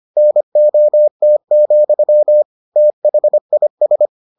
Morse code, heard as a keyed tone, about 25 wpm, 600 Hz. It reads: NOT, THIS